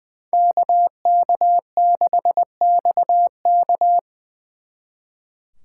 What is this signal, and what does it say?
Morse code, heard as a keyed tone, about 24 wpm, 700 Hz. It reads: KK6XK
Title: Speed Racing - USA 2x2 Call Signs 20wpm